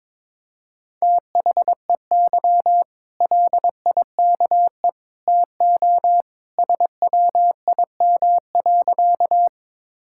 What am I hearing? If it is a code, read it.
Morse code, heard as a keyed tone, about 22 wpm, 700 Hz. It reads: THEY LIKE TO SWIM.
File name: practice / 22wpm / Easy Sentences with Hints - 22wpm